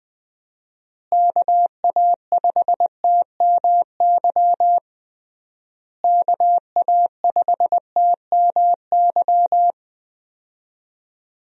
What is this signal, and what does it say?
Morse code, heard as a keyed tone, about 21 wpm, 700 Hz. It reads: KA5TMY KA5TMY
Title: USA Call Signs - 6 Characters - Repeated 3x - 20wpm